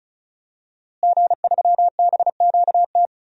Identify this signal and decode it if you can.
Morse code, heard as a keyed tone, about 36 wpm, 700 Hz. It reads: G3BQT
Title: DXCC-100 Call Signs - 5 Characters - Repeated 3x - 35wpm